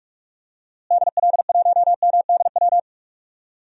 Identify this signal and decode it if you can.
Morse code, heard as a keyed tone, about 42 wpm, 700 Hz. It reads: DL1MDW
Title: DXCC-100 Call Signs - 6 Characters - Repeated 2x - 45wpm